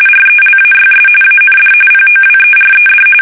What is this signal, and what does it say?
RTTY, about 45.45 baud. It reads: RELAYS AND POWERED
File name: fsk850 / teletype-corp